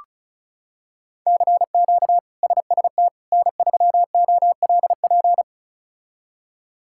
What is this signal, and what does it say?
Morse code, heard as a keyed tone, about 36 wpm, 700 Hz. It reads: CQ SST N3OLP
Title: K1USN SST QSO Practice - 35wpm